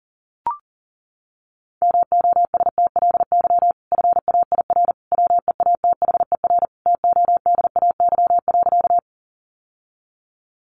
Morse code, heard as a keyed tone, about 40 wpm, 700 Hz. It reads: MOSTLY FAIR WEATHER TODAY.
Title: Sentences from Top 1300 Words 40wpm